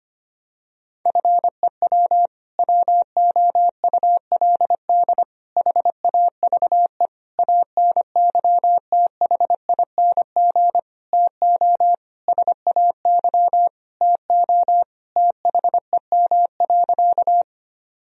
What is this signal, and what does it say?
Morse code, heard as a keyed tone, about 25 wpm, 700 Hz. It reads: FEW WOULD HAVE ANYTHING TO SAY TO THEM.
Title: Sentences from Top 1000 Words - Review Part 1 - 25wpm